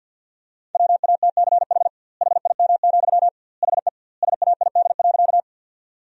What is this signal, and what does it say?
Morse code, heard as a keyed tone, about 50 wpm, 700 Hz. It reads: WATCH HIM, HE SAID.
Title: 2x Word-Spacing - Sentences from Top 300 Words - 50wpm